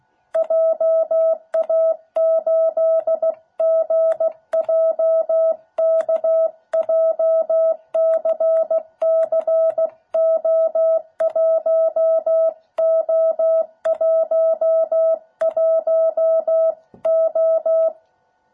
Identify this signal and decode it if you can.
Morse code, heard as a keyed tone, about 16 wpm, 660 Hz. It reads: JA8GJK JCCO1O11O